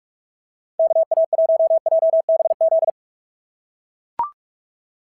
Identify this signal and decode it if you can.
Morse code, heard as a keyed tone, about 44 wpm, 650 Hz. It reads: KA1JBZ
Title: SOTA QSO Practice 45wpm